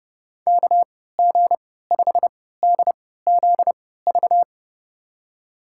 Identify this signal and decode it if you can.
Morse code, heard as a keyed tone, about 29 wpm, spaced as for 17 wpm, 700 Hz. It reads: KG5DZV
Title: USA Call Signs - 3x Character Spacing - 30wpm